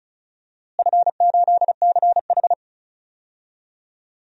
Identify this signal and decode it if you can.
Morse code, heard as a keyed tone, about 34 wpm, 700 Hz. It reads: F8CH
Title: DXCC-100 Call Signs - 4 Characters - Repeated 3x - 35wpm